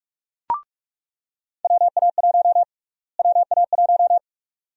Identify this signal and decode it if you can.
Morse code, heard as a keyed tone, about 46 wpm, 700 Hz. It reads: WA1 WA1
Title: USA Call Sign Prefixes - Repeated 2x - 45wpm